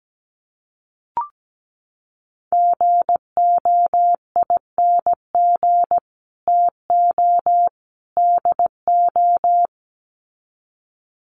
Morse code, heard as a keyed tone, about 17 wpm, 700 Hz. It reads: GOING TO DO
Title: Sets of 3 Words 17wpm